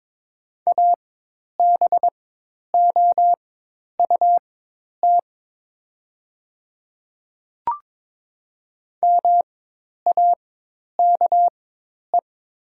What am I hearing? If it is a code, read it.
Morse code, heard as a keyed tone, about 22 wpm, spaced as for 10 wpm, 700 Hz. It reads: ABOUT MAKE
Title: 4x Character Spacing - Top 100 Words - 22wpm